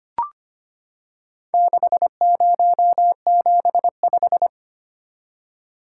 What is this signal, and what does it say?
Morse code, heard as a keyed tone, about 25 wpm, 700 Hz. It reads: 6075
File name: practice / 25wpm / Four Letter-Number Combinations - 25wpm